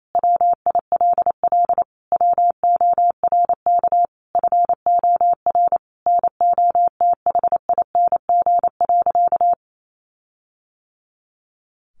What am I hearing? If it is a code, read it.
Morse code, heard as a keyed tone, about 28 wpm, 700 Hz. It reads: WILL WORK FOR NOTHING.
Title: Sentences from Top 1400 Words 28wpm